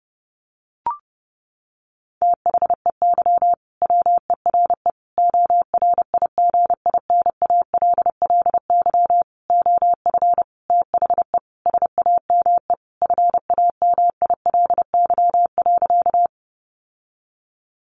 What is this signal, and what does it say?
Morse code, heard as a keyed tone, about 30 wpm, 700 Hz. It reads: THEY WERE ORIGINALLY OF THE SAME FAMILY.
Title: Sentences from Top 1500 Words 30wpm